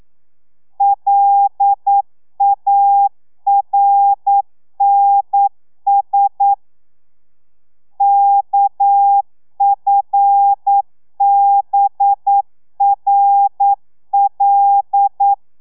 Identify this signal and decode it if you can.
Morse code, heard as a keyed tone, about 9 wpm, 800 Hz. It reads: LARNS KFBRL
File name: telegrafia / i2tfj / Lez6 40